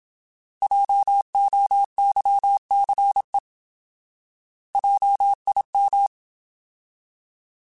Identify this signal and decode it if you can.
Morse code, heard as a keyed tone, about 27 wpm, 800 Hz. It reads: JOYCE JIM
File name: Alphabetical / Alphabetical mp3s / Js @25